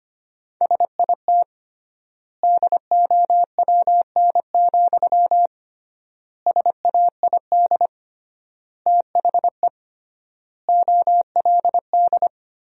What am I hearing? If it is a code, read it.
Morse code, heard as a keyed tone, about 25 wpm, 700 Hz. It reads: SIT DOWN, SAID THE OLD
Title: 3x Word-Spacing - Sentences from Top 500 Words - 25wpm